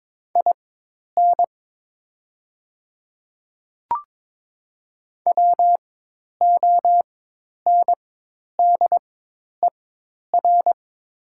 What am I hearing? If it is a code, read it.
Morse code, heard as a keyed tone, about 22 wpm, spaced as for 10 wpm, 700 Hz. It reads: IN WONDER
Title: 4x Character Spacing - Top 500 Words - 22wpm